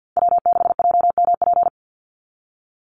Morse code, heard as a keyed tone, about 50 wpm, 700 Hz. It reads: W6JMP
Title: NA SOTA Chaser Callsigns 50wpm